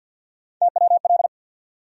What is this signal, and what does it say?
Morse code, heard as a keyed tone, about 52 wpm, 700 Hz. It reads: TWL
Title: USA Call Sign Postfixes - 3 Letters 50wpm